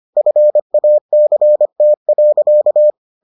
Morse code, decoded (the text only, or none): FACT.